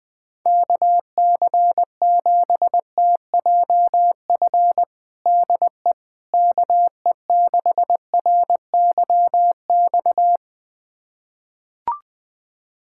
Morse code, heard as a keyed tone, about 20 wpm, 700 Hz. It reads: KC7TJF DE KE6RYX
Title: USA Callsign DE Callsign - 6 Characters - 20wpm